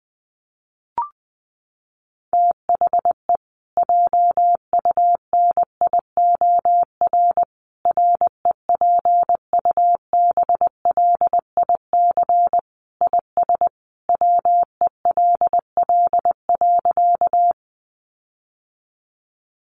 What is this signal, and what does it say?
Morse code, heard as a keyed tone, about 20 wpm, 700 Hz. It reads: THE JUNIOR REPUBLIC IS WELL.
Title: Sentences from Top 1500 Words 20wpm